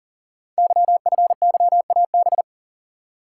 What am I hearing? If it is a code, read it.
Morse code, heard as a keyed tone, about 39 wpm, 700 Hz. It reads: YFYAB
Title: Code Groups - 5 Letter:Numbers - 40wpm